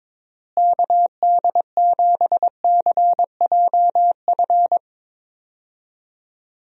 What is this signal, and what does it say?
Morse code, heard as a keyed tone, about 22 wpm, 700 Hz. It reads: KD7CJF
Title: USA Call Signs - 6 Characters - Repeated 3x - 22wpm